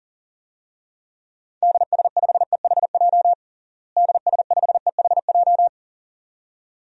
Morse code, heard as a keyed tone, about 40 wpm, 700 Hz. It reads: DS5EHJ DS5EHJ
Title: DXCC-100 Call Signs - 6 Characters - Repeated 2x - 40wpm